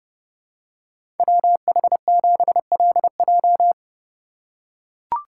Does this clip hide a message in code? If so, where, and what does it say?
Yes: Morse code from 1.2 to 3.7 s, heard as a keyed tone, about 30 wpm, 700 Hz. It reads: WH7LJ